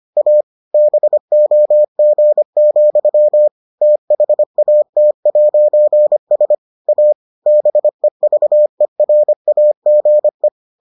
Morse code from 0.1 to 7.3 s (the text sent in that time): A BOG, THAT'S A